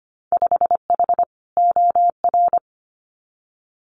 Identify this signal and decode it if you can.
Morse code, heard as a keyed tone, about 25 wpm, 690 Hz. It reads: 5H OR
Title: Winter Field Day Exchanges Encore 25wpm